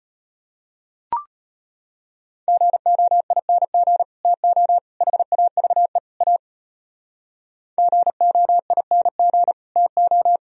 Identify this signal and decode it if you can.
Morse code, heard as a keyed tone, about 31 wpm, 700 Hz. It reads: GOING TO HAVE A GOING TO